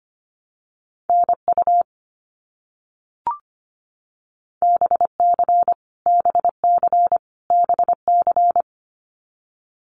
Morse code, heard as a keyed tone, about 25 wpm, 700 Hz. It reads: NU BC BC BC
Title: Canadian Provinces and Territories - Repeated 3x - 25wpm